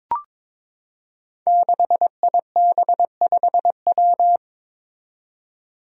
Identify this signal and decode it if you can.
Morse code, heard as a keyed tone, about 22 wpm, 700 Hz. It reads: 6IB5W
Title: Five Challenging Letter-Number Combinations 22wpm